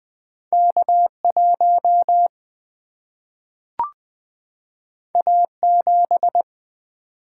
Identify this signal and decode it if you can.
Morse code, heard as a keyed tone, about 20 wpm, 700 Hz. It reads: K 1 A 7
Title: Single-Letter Single-Number 20wpm